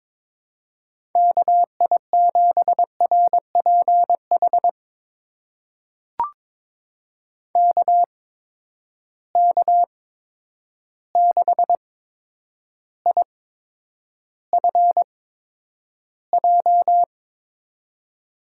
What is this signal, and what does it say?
Morse code, heard as a keyed tone, about 22 wpm, spaced as for 9 wpm, 700 Hz. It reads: KI7RPH KK6IFJ
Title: USA Call Signs - 6 Characters - 8x Character Spacing - 22wpm